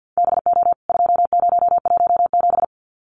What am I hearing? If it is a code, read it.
Morse code, heard as a keyed tone, about 53 wpm, 700 Hz. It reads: BO 2017
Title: CWT Exchange 50wpm